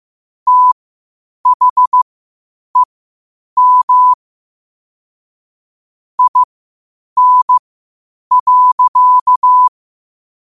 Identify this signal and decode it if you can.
Morse code, heard as a keyed tone, about 15 wpm, spaced as for 8 wpm, 1000 Hz. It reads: THEM IN.
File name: qotdfiles / 8wpm